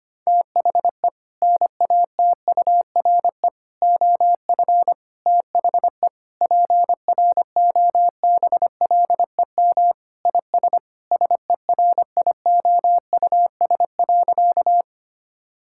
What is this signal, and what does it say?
Morse code, heard as a keyed tone, about 25 wpm, 700 Hz. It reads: THE NATURE OF THE PROBLEM IS SERIOUS.
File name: practice / 25wpm / Sentences with 3-Word Phrases - 25wpm